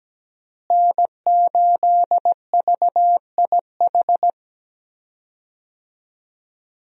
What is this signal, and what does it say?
Morse code, heard as a keyed tone, about 17 wpm, 700 Hz. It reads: N8VIH
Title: USA Call Signs - 5 Characters - Repeated 2x - 17wpm